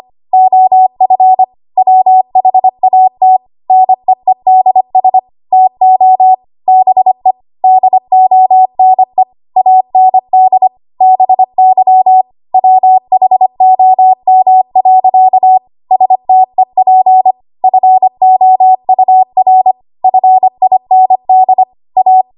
Morse code, heard as a keyed tone, about 25 wpm, 750 Hz. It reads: OF WHAT NEEDS TO BE DONE AND BY WHOM. STEP FOUR FIND A
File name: Morse / 25WPM